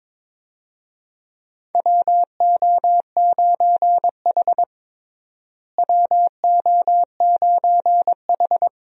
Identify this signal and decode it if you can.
Morse code, heard as a keyed tone, about 21 wpm, 700 Hz. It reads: WO9H WO9H
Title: USA Call Signs - 4 Characters - Repeated 3x - 22wpm